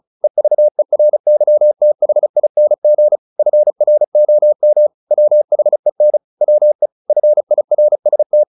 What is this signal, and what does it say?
Morse code, heard as a keyed tone, about 34 wpm, 600 Hz. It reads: EVERYTHING FROM WHEN WE FIRST
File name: the old man and the sea / the old man and the sea 1 35wpm